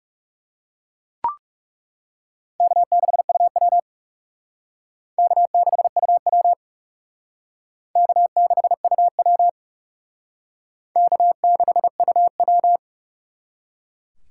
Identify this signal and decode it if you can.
Morse code, heard as a keyed tone, about 36 wpm, 700 Hz. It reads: K6UW K6UW K6UW K6UW
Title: Speed Racing - USA Call Signs 30wpm